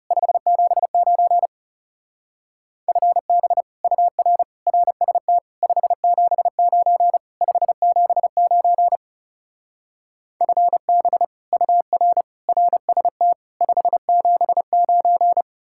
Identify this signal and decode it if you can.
Morse code, heard as a keyed tone, about 39 wpm, 700 Hz. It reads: 579 FB UR RST 579 579 FB UR RST 579